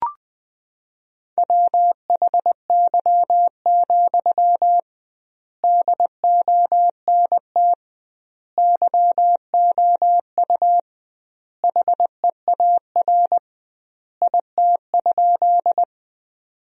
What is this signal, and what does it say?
Morse code, heard as a keyed tone, about 21 wpm, 700 Hz. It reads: WHY, DONT YOU HEAR IT?
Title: 2x Word-Spacing - Sentences from Top 300 Words - 20wpm